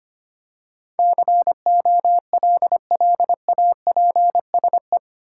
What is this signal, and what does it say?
Morse code, heard as a keyed tone, about 25 wpm, 700 Hz. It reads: COLLAPSE
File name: Eight Letter Words - 25wpm